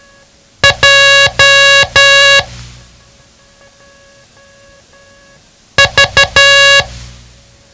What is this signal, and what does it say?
Morse code, heard as a keyed tone, about 12 wpm, 1650 Hz. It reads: J V
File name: ham radio / CodeQuick / Code Quick 07